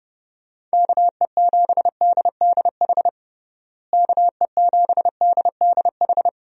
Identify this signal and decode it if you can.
Morse code, heard as a keyed tone, about 32 wpm, 700 Hz. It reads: KE7DDH KE7DDH
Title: USA Call Signs - 6 Characters - Repeated 3x - 30wpm